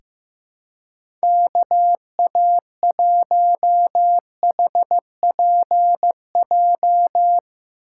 Morse code, heard as a keyed tone, about 15 wpm, 700 Hz. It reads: KA1HPJ
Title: USA Call Signs - 2x Character Spacing - 15wpm